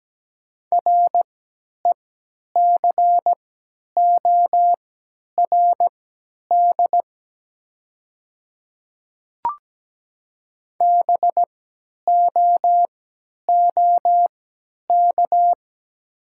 Morse code, heard as a keyed tone, about 17 wpm, spaced as for 10 wpm, 700 Hz. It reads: RECORD BOOK
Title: 3x Character Spacing - Top 500 Words - 17wpm